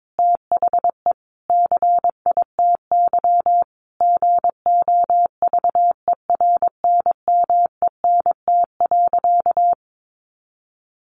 Morse code, heard as a keyed tone, about 22 wpm, 700 Hz. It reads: THE CITY GOVERNMENT.
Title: Sentences from Top 1400 Words 22wpm